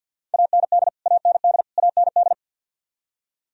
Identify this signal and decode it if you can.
Morse code, heard as a keyed tone, about 51 wpm, 700 Hz. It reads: AND AND AND
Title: Top 100 Words Repeated 50wpm